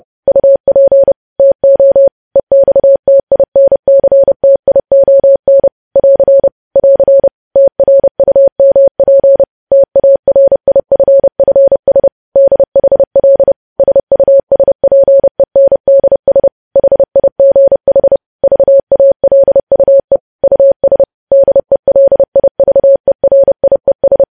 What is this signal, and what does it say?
Morse code, heard as a keyed tone, about 30 wpm, 550 Hz. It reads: UP TO EXTINCTION <AR> <AR> TRUMP TARIFFS DHL SUSPENDS HIGH VALUE US DELIVERIES
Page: news headlines in morse code at 30 wpm